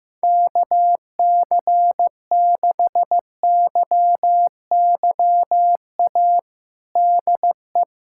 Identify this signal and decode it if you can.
Morse code, heard as a keyed tone, about 15 wpm, 700 Hz. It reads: KC6YYA DE
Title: USA Callsign DE Callsign - 6 Characters - 15wpm